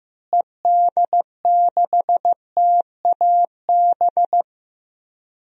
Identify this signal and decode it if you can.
Morse code, heard as a keyed tone, about 15 wpm, 700 Hz. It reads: ED6TAB
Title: DXCC-100 Call Signs - 6 Characters - Repeated 3x - 15wpm